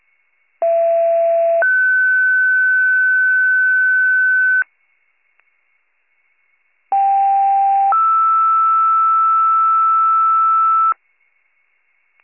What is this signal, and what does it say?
Two-tone page, heard as a one-second tone, then a three-second tone, about 670 then 1500 Hz, about 790 then 1320 Hz. It is heard from 0.6 to 10.9 s.